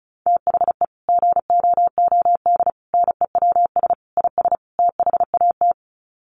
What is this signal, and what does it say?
Morse code, heard as a keyed tone, about 34 wpm, 700 Hz. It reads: THE GOOD NEWS IS THAT
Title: Sets of 5 Words 35wpm